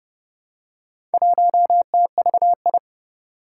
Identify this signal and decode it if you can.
Morse code, heard as a keyed tone, about 30 wpm, 700 Hz. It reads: 1TVI
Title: Four Challenging Letter-Number Combinations 30wpm